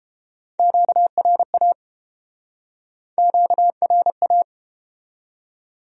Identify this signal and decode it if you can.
Morse code, heard as a keyed tone, about 31 wpm, 700 Hz. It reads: QRA QRA